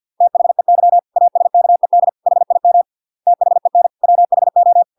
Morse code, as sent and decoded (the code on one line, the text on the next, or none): - .... -.-- / .- ... -.- . -.. / .... .. -- / - .... -. / .-- .... ---
THEY ASKED HIM THEN WHO